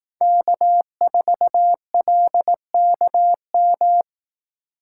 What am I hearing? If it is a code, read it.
Morse code, heard as a keyed tone, about 18 wpm, 700 Hz. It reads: K4LKM
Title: Speed Racing - USA 1x3 Call Signs 15wpm